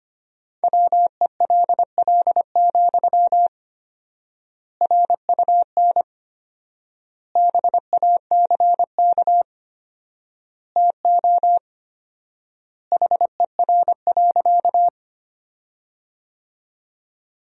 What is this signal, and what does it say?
Morse code, heard as a keyed tone, about 25 wpm, 700 Hz. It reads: WELL, RUN BACK TO HER.
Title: 4x Word-Spacing - Sentences from Top 300 Words - 25wpm